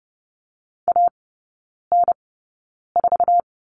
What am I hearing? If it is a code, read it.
Morse code, heard as a keyed tone, about 30 wpm, spaced as for 9 wpm, 700 Hz. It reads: AN4
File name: Lesson 18e - Letter m - Three Characters 30wpm